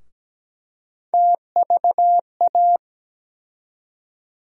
Morse code, heard as a keyed tone, about 17 wpm, 700 Hz. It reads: TVA
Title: USA Call Sign Postfixes - 3 Letters 17wpm